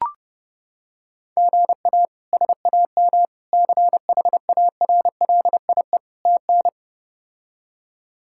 Morse code, heard as a keyed tone, about 32 wpm, 700 Hz. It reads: GA SAM CHARLIE TN
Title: K1USN SST QSO Practice - 30wpm